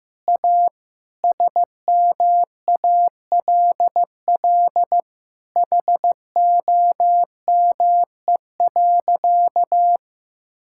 Morse code, heard as a keyed tone, about 16 wpm, 700 Hz. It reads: A SMALL HOME.
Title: Sentences from Top 1000 Words - Review Part 6 - 15wpm